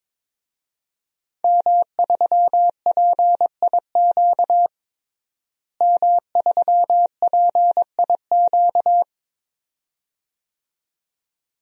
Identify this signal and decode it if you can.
Morse code, heard as a keyed tone, about 21 wpm, 700 Hz. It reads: M3PIQ M3PIQ
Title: DXCC-100 Call Signs - 5 Characters - Repeated 2x - 22wpm